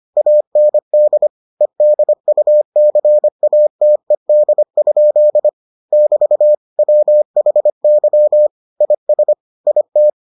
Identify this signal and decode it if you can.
Morse code, heard as a keyed tone, about 25 wpm, 600 Hz. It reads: AND EDUCATED? <BT> WHY IS IT